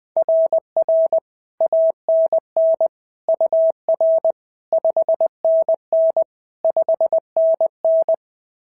Morse code, heard as a keyed tone, about 20 wpm, 650 Hz. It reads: RR ANN UR 5NN 5NN
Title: SOTA QSO Practice 20wpm